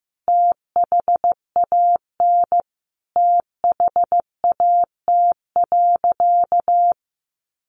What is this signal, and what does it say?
Morse code, heard as a keyed tone, about 14 wpm, 700 Hz. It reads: THAN THAT.
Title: Sentences from Top 800 Words 15wpm